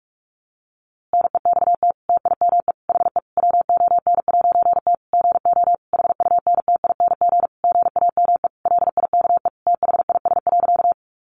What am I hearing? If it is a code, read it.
Morse code, heard as a keyed tone, about 44 wpm, 700 Hz. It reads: NEXT TIME HE WON'T GO HUNTING GAME LIKE THIS.
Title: A Sound Of Thunder 45wpm